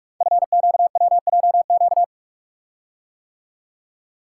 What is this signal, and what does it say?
Morse code, heard as a keyed tone, about 44 wpm, 700 Hz. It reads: FQWJQ
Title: Code Groups - 5 Letters - 45wpm